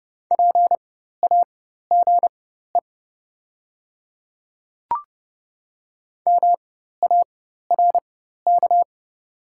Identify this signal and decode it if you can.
Morse code, heard as a keyed tone, about 30 wpm, spaced as for 14 wpm, 700 Hz. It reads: PAGE MARK
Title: 4x Character Spacing - Top 300 Words - 30wpm